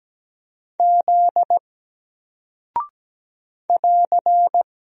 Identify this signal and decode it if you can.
Morse code, heard as a keyed tone, about 17 wpm, 700 Hz. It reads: Z <AR>
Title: Single Character (Letter:Number:Punctuation:Prosign) 17wpm